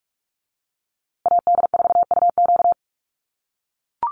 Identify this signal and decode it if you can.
Morse code, heard as a keyed tone, about 44 wpm, 700 Hz. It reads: AD4UQ
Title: USA Call Signs Repeated 3x 45wpm